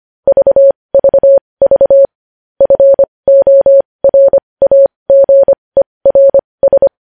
Morse code, heard as a keyed tone, about 25 wpm, spaced as for 20 wpm, 550 Hz. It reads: VVV FORAGERS
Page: news headlines in morse code at 20 wpm